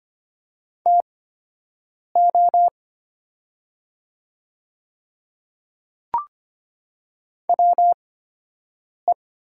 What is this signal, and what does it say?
Morse code, heard as a keyed tone, about 25 wpm, spaced as for 7 wpm, 700 Hz. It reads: TO WE
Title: 8x Character Spacing - Top 100 Words - 25wpm